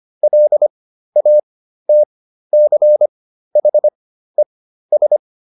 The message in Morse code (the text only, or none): LATCHES